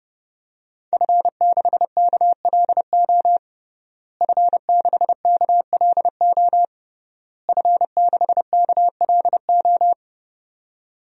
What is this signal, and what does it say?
Morse code, heard as a keyed tone, about 32 wpm, 700 Hz. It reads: F6KLO F6KLO F6KLO
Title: DXCC-100 Call Signs - 5 Characters - Repeated 3x - 30wpm